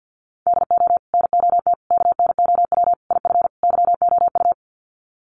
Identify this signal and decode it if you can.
Morse code, heard as a keyed tone, about 50 wpm, 700 Hz. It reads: DO NOT KNOW IF YOU
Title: Sets of 5 Words 50wpm